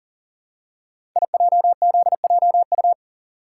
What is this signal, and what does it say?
Morse code, heard as a keyed tone, about 41 wpm, 690 Hz. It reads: IJZJU
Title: Code Groups - 5 Letter:Numbers - 40wpm